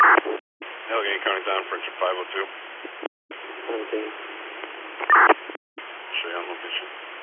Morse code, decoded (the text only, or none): none